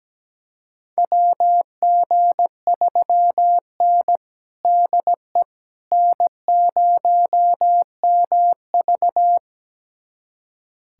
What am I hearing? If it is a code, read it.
Morse code, heard as a keyed tone, about 17 wpm, 700 Hz. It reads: WG3N DE N0MV
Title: USA Callsign DE Callsign - 4 Characters - 17wpm